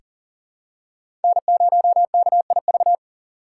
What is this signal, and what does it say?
Morse code, heard as a keyed tone, about 40 wpm, 700 Hz. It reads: N0KIV
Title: USA Call Signs - 5 Characters - 4x Character Spacing - 40wpm